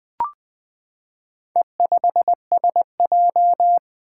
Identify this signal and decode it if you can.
Morse code, heard as a keyed tone, about 20 wpm, 700 Hz. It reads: E5SJ
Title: Four Letter-Number Combinations - 20wpm